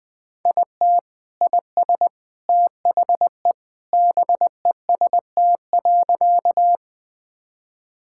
Morse code, heard as a keyed tone, about 20 wpm, 700 Hz. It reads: IT IS THE BEST.
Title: Sentences with 3-Word Phrases - Spoken Beginning - 20wpm